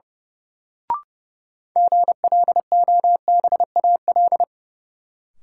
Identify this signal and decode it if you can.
Morse code, heard as a keyed tone, about 30 wpm, 700 Hz. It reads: GLOBAL
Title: Sentences from Top 1200 Words - New Word Drill Alt - 30wpm